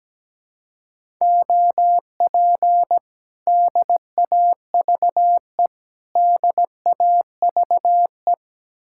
Morse code, heard as a keyed tone, about 17 wpm, 700 Hz. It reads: OP DAVE DAVE
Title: Speed-Racing Common QSO Phrases Encore - 17wpm